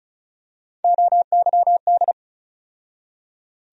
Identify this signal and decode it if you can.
Morse code, heard as a keyed tone, about 36 wpm, 700 Hz. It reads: OYD
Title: USA Call Sign Postfixes - 3 Letters 35wpm